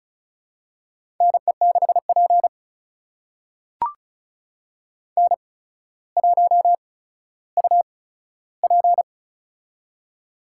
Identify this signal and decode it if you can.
Morse code, heard as a keyed tone, about 36 wpm, spaced as for 15 wpm, 700 Hz. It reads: NE6P N1UP
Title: USA Call Signs - 8x Character Spacing - 35wpm